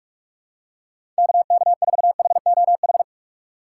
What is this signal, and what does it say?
Morse code, heard as a keyed tone, about 41 wpm, 700 Hz. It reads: KK4HOH
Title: USA Call Signs - 6 Characters - 4x Character Spacing - 45wpm